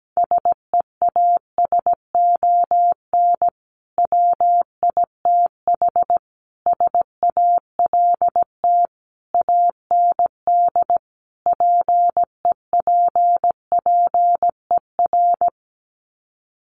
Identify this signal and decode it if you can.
Morse code, heard as a keyed tone, about 17 wpm, 700 Hz. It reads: SEASON WITH SALT AND PEPPER
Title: Sets of 5 Words 17wpm